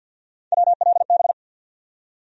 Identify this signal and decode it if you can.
Morse code, heard as a keyed tone, about 49 wpm, 700 Hz. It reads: WLB
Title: USA Call Sign Postfixes - 3 Letters 50wpm